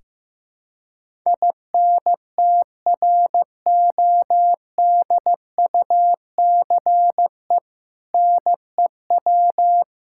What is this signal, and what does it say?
Morse code, heard as a keyed tone, about 15 wpm, 700 Hz. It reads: INTRODUCE NEW
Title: Sentences from Top 1600 Words 15wpm